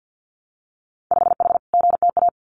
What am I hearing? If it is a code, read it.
Morse code, heard as a keyed tone, about 50 wpm, 700 Hz. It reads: 5H GTA
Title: Winter Field Day Exchanges Encore 50wpm